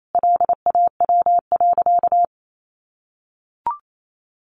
Morse code, heard as a keyed tone, about 28 wpm, 700 Hz. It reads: LAW.
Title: Sentences from Top 1400 Words 28wpm